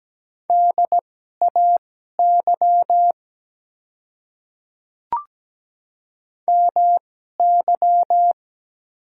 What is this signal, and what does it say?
Morse code, heard as a keyed tone, about 17 wpm, spaced as for 12 wpm, 700 Hz. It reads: DAY MY
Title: 2x Character Spacing - Top 100 Words - 17wpm